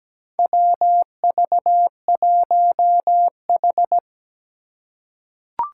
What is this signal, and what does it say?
Morse code, heard as a keyed tone, about 17 wpm, 700 Hz. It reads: WV1H
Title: Four Challenging Letter-Number Combinations 17wpm